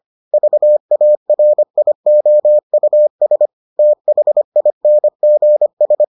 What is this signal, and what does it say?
Morse code, heard as a keyed tone, about 25 wpm, 600 Hz. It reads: VARIOUS THINGS